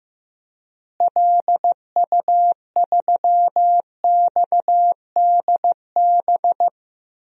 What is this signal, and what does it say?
Morse code, heard as a keyed tone, about 15 wpm, 700 Hz. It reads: LU3XDB
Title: DXCC-100 Call Signs - 6 Characters - Repeated 3x - 15wpm